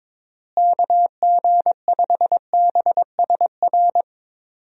Morse code, heard as a keyed tone, about 22 wpm, 700 Hz. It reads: KG5BSR
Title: USA Call Signs - 6 Characters - Repeated 3x - 22wpm